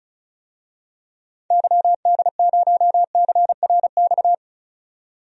Morse code, heard as a keyed tone, about 36 wpm, 700 Hz. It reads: YD0CRX
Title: DXCC-100 Call Signs - 6 Characters - 35wpm